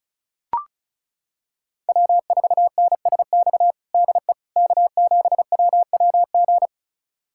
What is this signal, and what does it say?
Morse code, heard as a keyed tone, about 34 wpm, 700 Hz. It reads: W4NSX DE K7WWG
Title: USA Callsign DE Callsign 35wpm